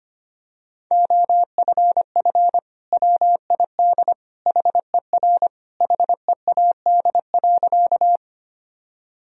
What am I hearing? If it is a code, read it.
Morse code, heard as a keyed tone, about 25 wpm, 700 Hz. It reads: OFF WID HER HEAD.
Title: Sentences from Top 300 Words and Abbreviations 25wpm